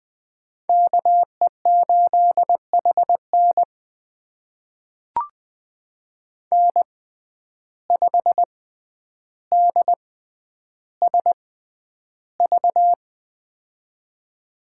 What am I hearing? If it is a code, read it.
Morse code, heard as a keyed tone, about 20 wpm, spaced as for 10 wpm, 700 Hz. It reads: KE8HN N5DSV
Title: USA Call Signs - 5 Characters - 6x Character Spacing - 20wpm